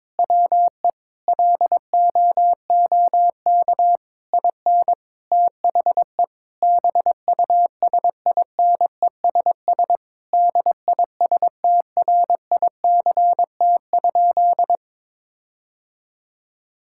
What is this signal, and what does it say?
Morse code, heard as a keyed tone, about 22 wpm, 700 Hz. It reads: WE LOOK IN THE BUSINESS DISTRICT?
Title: Sentences from Top 1600 Words 22wpm